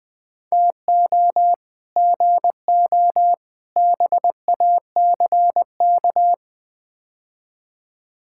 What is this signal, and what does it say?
Morse code, heard as a keyed tone, about 20 wpm, 700 Hz. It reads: TO GO BACK
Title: Build A Phrase - 5-Word Phrases - 20wpm